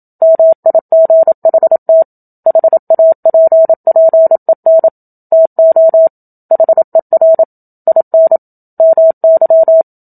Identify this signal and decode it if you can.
Morse code, heard as a keyed tone, about 27 wpm, 650 Hz. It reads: MIGHT HAPPEN TO HER IN MY
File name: Book Two / 01-Chapter 1 - 27 WPM